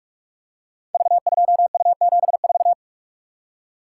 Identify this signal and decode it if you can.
Morse code, heard as a keyed tone, about 44 wpm, 700 Hz. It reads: V2U74
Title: Code Groups - 5 Letter:Numbers - 45wpm